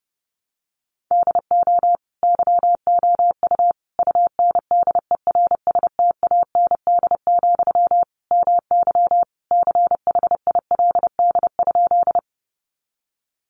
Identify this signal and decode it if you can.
Morse code, heard as a keyed tone, about 30 wpm, 700 Hz. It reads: DO YOU UNDERSTAND, MY CHILD?
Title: Sentences from Top 800 Words 30wpm